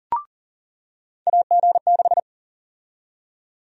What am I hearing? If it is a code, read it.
Morse code, heard as a keyed tone, about 40 wpm, 700 Hz. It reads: AG6
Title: USA Call Sign Prefixes - 40wpm